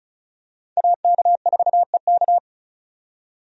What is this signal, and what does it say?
Morse code, heard as a keyed tone, about 36 wpm, 700 Hz. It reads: AK4EK
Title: USA Call Signs - 5 Characters - 35wpm